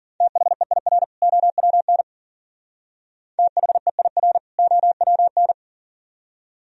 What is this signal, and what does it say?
Morse code, heard as a keyed tone, about 50 wpm, 700 Hz. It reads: THEIR OWN THEIR OWN